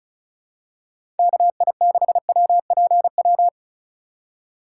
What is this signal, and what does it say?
Morse code, heard as a keyed tone, about 36 wpm, 700 Hz. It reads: KI6WPW
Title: USA Call Signs - 2x Character Spacing - 35wpm